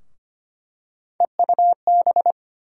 Morse code, heard as a keyed tone, about 25 wpm, 700 Hz. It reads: EUB
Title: USA Call Sign Postfixes - 3 Letters 25wpm